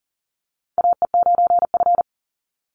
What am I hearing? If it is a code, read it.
Morse code, heard as a keyed tone, about 40 wpm, 700 Hz. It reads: AE9F
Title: NA SOTA Chaser Callsigns 40wpm